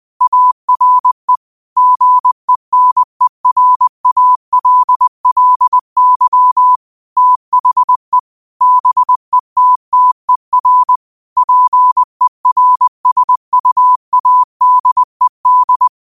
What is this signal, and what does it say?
Morse code, heard as a keyed tone, about 20 wpm, 1000 Hz. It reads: ARE GENERALLY THE BETTER PERSUADED